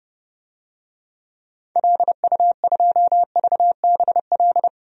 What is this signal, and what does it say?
Morse code, heard as a keyed tone, about 30 wpm, 700 Hz. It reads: LU2VBL
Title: DXCC-100 Call Signs - 6 Characters - 30wpm